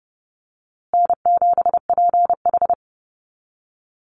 Morse code, heard as a keyed tone, about 30 wpm, 700 Hz. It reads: N7PH
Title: USA Call Signs Repeated 3x 30wpm